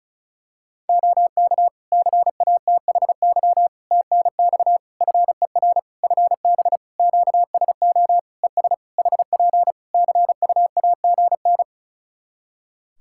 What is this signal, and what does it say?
Morse code, heard as a keyed tone, about 36 wpm, 700 Hz. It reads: OK CATHY TNX FER FB QSO ES HP CUAGN
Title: Practice QSO - Exchange 4 - Ending - 35wpm